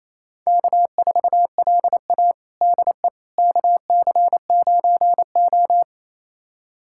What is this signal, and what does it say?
Morse code, heard as a keyed tone, about 28 wpm, 700 Hz. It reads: K4LA DE KC9O
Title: USA Callsign DE Callsign - 4 Characters - 28wpm